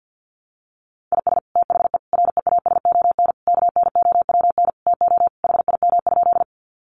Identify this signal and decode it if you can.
Morse code, heard as a keyed tone, about 50 wpm, 700 Hz. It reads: IS THE REASON KNOWN TO HIM?
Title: Sentences from Top 700 Words 50wpm